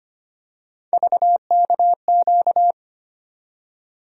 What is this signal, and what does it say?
Morse code, heard as a keyed tone, about 25 wpm, 700 Hz. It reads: VKQ
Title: USA Call Sign Postfixes - 3 Letters 25wpm